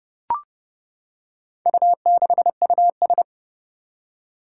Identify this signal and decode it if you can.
Morse code, heard as a keyed tone, about 30 wpm, 700 Hz. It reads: U6US
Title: Four Challenging Letter-Number Combinations 30wpm